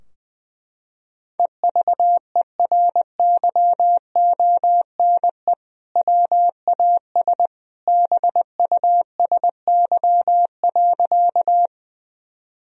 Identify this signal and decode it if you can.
Morse code, heard as a keyed tone, about 20 wpm, 700 Hz. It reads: EVERYONE WAS BUSY.
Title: Sentences from Top 1000 Words - Review Part 3 - 20wpm